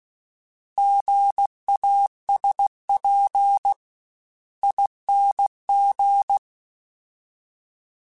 Morse code, heard as a keyed tone, about 16 wpm, 800 Hz. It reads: GASP ING